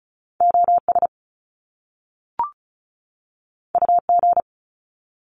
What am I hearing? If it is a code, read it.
Morse code, heard as a keyed tone, about 36 wpm, 700 Hz. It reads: OS UG
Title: Top Two Letter Combinations in Words 35wpm